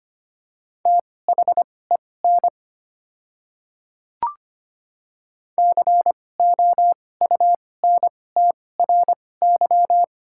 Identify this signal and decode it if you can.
Morse code, heard as a keyed tone, about 25 wpm, spaced as for 18 wpm, 690 Hz. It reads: THEN COUNTRY
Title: 2x Character Spacing - Top 300 Words - 25wpm